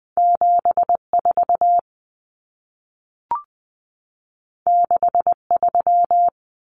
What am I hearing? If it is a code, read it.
Morse code, heard as a keyed tone, about 20 wpm, 700 Hz. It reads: 74 63
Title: Common Cut Numbers - 2-Digit 20wpm